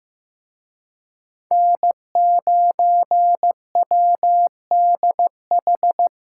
Morse code, heard as a keyed tone, about 18 wpm, 700 Hz. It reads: N9WDH
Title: Speed Racing - USA 1x3 Call Signs 15wpm